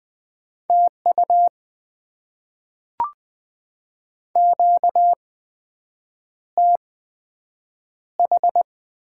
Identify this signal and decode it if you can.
Morse code, heard as a keyed tone, about 20 wpm, spaced as for 7 wpm, 700 Hz. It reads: TU QTH
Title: Top 20 QSO Elements - 8x Character Spacing - 20wpm